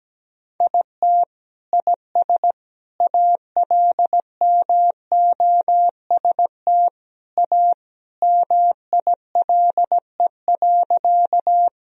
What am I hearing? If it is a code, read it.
Morse code, heard as a keyed tone, about 17 wpm, 700 Hz. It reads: IT IS ALMOST A MILE.
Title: Sentences from Top 1000 Words - Review Part 4 - 17wpm